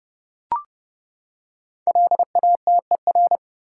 Morse code, heard as a keyed tone, about 30 wpm, 700 Hz. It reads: LATER
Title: Five Letter Words - 30wpm